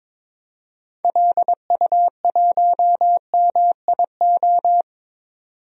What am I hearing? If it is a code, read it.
Morse code, heard as a keyed tone, about 22 wpm, 700 Hz. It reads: LU1MIO
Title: DXCC-100 Call Signs - 6 Characters - 22wpm